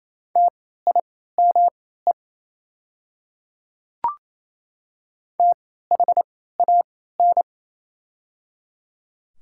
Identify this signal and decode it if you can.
Morse code, heard as a keyed tone, about 27 wpm, spaced as for 16 wpm, 700 Hz. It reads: TIME THAN
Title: 3x Character Spacing - Top 300 Words - 28wpm